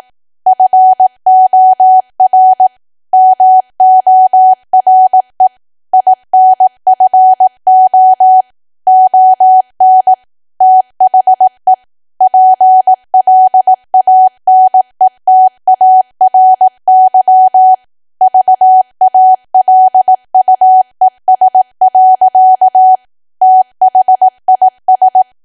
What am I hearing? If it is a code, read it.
Morse code, heard as a keyed tone, about 18 wpm, 750 Hz. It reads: FOR MORE INFO ON THE PLANETARY VALUES. THIS
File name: cwa / W1AW-Level2 / 18WPM